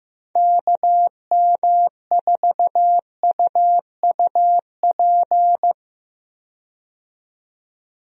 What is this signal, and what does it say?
Morse code, heard as a keyed tone, about 15 wpm, 700 Hz. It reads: KM4UUP
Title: USA Call Signs - 6 Characters - 15wpm